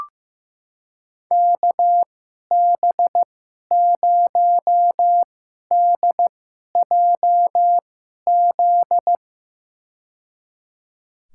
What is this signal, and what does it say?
Morse code, heard as a keyed tone, about 15 wpm, spaced as for 11 wpm, 700 Hz. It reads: KB0DJZ
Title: USA Call Signs - 2x Character Spacing - 15wpm